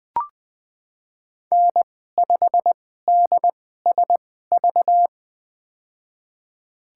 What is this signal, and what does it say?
Morse code, heard as a keyed tone, about 20 wpm, spaced as for 14 wpm, 700 Hz. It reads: N5DSV
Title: USA Call Signs - 2x Character Spacing - 20wpm